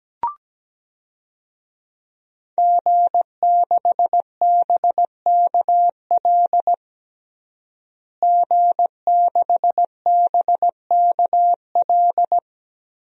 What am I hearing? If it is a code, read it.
Morse code, heard as a keyed tone, about 17 wpm, 700 Hz. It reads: G6BKL G6BKL
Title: DXCC-100 Call Signs - 5 Characters - Repeated 3x - 17wpm